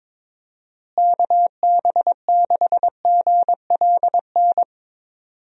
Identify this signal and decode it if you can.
Morse code, heard as a keyed tone, about 21 wpm, 700 Hz. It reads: KB6GLN
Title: USA Call Signs - 6 Characters - Repeated 3x - 22wpm